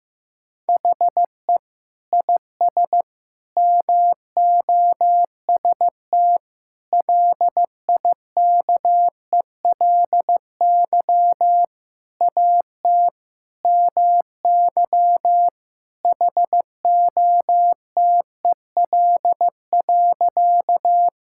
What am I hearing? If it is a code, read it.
Morse code, heard as a keyed tone, about 15 wpm, 700 Hz. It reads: HE IS MOST LIKELY AT MY HOTEL.
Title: Sentences from Top 1600 Words 15wpm